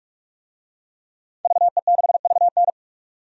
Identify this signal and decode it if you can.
Morse code, heard as a keyed tone, about 45 wpm, 700 Hz. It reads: VE6VN
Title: DXCC-100 Call Signs - 5 Characters - 45wpm